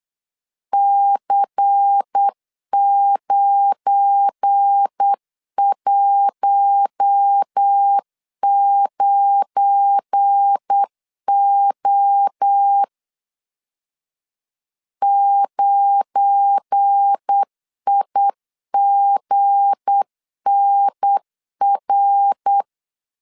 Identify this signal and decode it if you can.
Morse code, heard as a keyed tone, about 8 wpm, 800 Hz. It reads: C919O 9IGNR
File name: LEZIONI / 7 GF9Z / 7 GF9 8wpm 3545